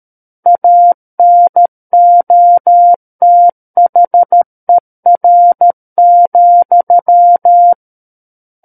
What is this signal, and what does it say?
Morse code, heard as a keyed tone, about 13 wpm, 700 Hz. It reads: ANOTHER,
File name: Book One / 04-Chapter 4 - 13 WPM